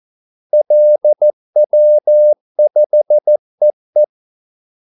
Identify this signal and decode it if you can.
Morse code, heard as a keyed tone, about 14 wpm, 600 Hz. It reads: LW5EE
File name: Callsigns-14wpm